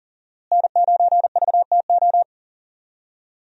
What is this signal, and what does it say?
Morse code, heard as a keyed tone, about 38 wpm, 700 Hz. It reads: N9VTO
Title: USA Call Signs - 5 Characters - Repeated 3x - 40wpm